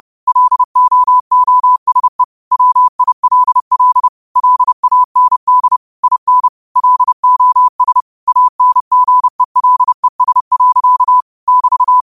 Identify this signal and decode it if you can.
Morse code, heard as a keyed tone, about 30 wpm, 1000 Hz. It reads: LOOSE WILL LAND IN LOS ANGELES. <BT>
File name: qotdfiles / 30wpm